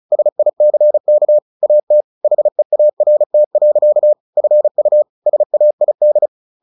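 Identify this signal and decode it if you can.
Morse code, heard as a keyed tone, about 34 wpm, 600 Hz. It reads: SICK AT HEART. FU SAID
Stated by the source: Morse practice at 35 wpm from CW med Gnister, sending Ken Follett's novel Never in English.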